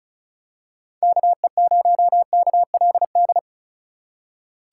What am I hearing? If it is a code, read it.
Morse code, heard as a keyed tone, about 36 wpm, 700 Hz. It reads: KE0KLD
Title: USA Call Signs - 3x Character Spacing - 35wpm